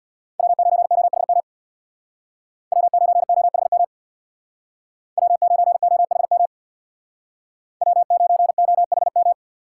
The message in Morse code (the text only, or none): W9OHM W9OHM W9OHM W9OHM